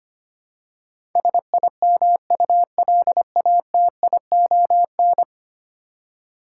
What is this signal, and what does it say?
Morse code, heard as a keyed tone, about 25 wpm, 700 Hz. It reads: SIMULATION
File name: 10 Letter Words - First 3 Letters Given - 25wpm